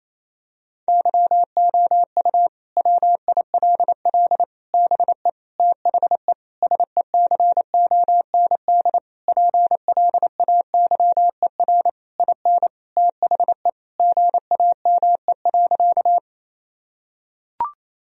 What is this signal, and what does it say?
Morse code, heard as a keyed tone, about 28 wpm, 700 Hz. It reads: YOU WILL BE THE SECOND PLAYER IN THE GAME.
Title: Sentences from Top 1000 Words - Review Part 4 - 28wpm